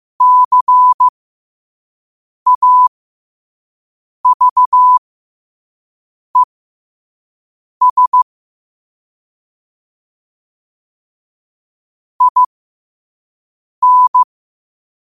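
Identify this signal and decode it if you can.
Morse code, heard as a keyed tone, about 15 wpm, spaced as for 5 wpm, 1000 Hz. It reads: CAVES IN
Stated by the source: Morse code sent at 5 wpm.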